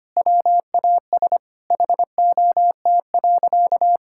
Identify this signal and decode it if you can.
Morse code, heard as a keyed tone, about 25 wpm, 700 Hz. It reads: WAS HOT.
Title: Sentences from Top 1000 Words - Review Part 5 - 25wpm